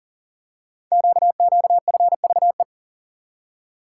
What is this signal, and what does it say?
Morse code, heard as a keyed tone, about 39 wpm, 690 Hz. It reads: QQFVE
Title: Code Groups - 5 Letter:Numbers - 40wpm